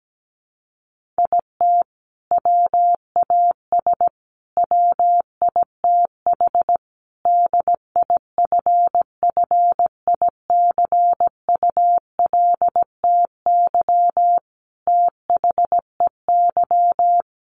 Morse code, heard as a keyed tone, about 17 wpm, 700 Hz. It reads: IT WAS WITH DIFFICULTY THEY
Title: Sentences from Top 1500 Words 17wpm